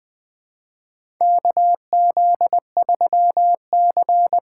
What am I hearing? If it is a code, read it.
Morse code, heard as a keyed tone, about 20 wpm, 700 Hz. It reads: KZ3C
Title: USA Call Signs - 3x Character Spacing - 20wpm